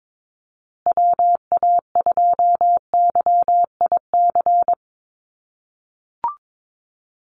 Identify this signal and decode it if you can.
Morse code, heard as a keyed tone, about 22 wpm, 700 Hz. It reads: WA2YIC
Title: USA Call Signs Repeated 3x 22wpm